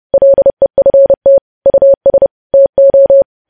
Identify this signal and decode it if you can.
Morse code, heard as a keyed tone, about 30 wpm, 550 Hz. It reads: LEFT US TO
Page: news headlines in morse code at 30 wpm